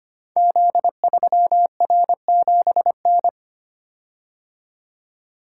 Morse code, heard as a keyed tone, about 25 wpm, 700 Hz. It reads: Z3R7N
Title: Code Groups - 5 Letter:Numbers - 25wpm